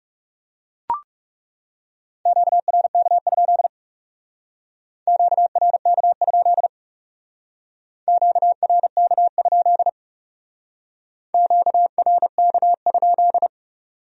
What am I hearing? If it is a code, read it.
Morse code, heard as a keyed tone, about 37 wpm, 700 Hz. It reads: QRK? QRK? QRK? QRK?
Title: Speed-Racing Q-Codes - 30wpm